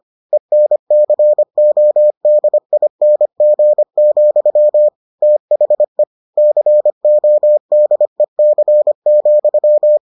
Morse code, heard as a keyed tone, about 25 wpm, 600 Hz. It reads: ENCODING, THE CODEC,